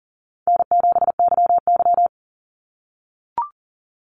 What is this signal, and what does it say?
Morse code, heard as a keyed tone, about 40 wpm, 700 Hz. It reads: N7YY
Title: NA SOTA Chaser Callsigns 40wpm